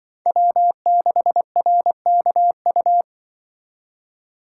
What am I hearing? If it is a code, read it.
Morse code, heard as a keyed tone, about 24 wpm, 700 Hz. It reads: W6RKU
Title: Speed Racing - USA 1x3 Call Signs 20wpm